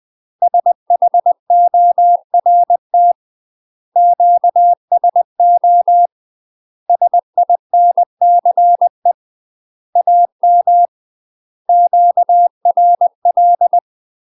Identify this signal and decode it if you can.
Morse code, heard as a keyed tone, about 20 wpm, 700 Hz. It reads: SHORT QSO SINCE AM QRL
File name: UK MASTER QSOs TRAINING / 20WPM / UK MASTER QSOs 0035